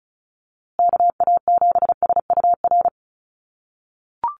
Morse code, heard as a keyed tone, about 34 wpm, 700 Hz. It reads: KA7SUR
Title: USA Call Signs Repeated 2x 35wpm